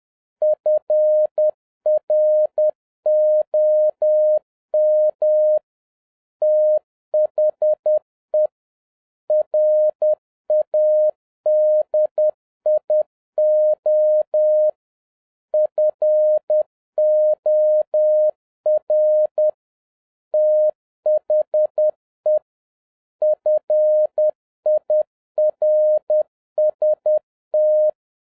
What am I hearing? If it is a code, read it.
Morse code, heard as a keyed tone, about 10 wpm, 610 Hz. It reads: FROM THE RADIO FOR THE FIRST